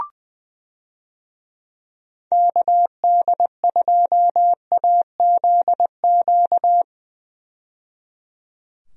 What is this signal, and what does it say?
Morse code, heard as a keyed tone, about 20 wpm, 700 Hz. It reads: KD2AZQ
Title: USA Call Signs - 6 Characters - 20wpm